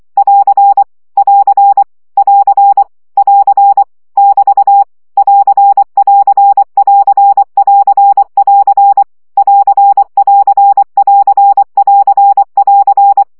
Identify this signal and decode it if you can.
Morse code, heard as a keyed tone, about 24 wpm, 800 Hz. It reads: <AR> <AR> <AR> <AR> <BT> <AR><AR><AR><AR><AR> <AR><AR><AR><AR><AR>
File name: CW gyakorlas plusz 20 baud